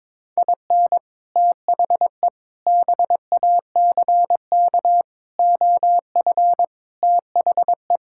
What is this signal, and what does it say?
Morse code, heard as a keyed tone, about 22 wpm, 700 Hz. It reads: IN THE BACK OF THE
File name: practice / 22wpm / Sets of 6 Words - 22wpm